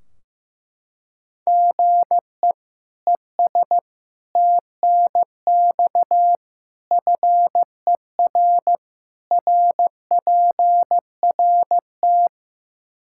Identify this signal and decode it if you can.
Morse code, heard as a keyed tone, about 15 wpm, 700 Hz. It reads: GE ES TNX FER RPRT
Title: K1USN SST QSO Practice 15wpm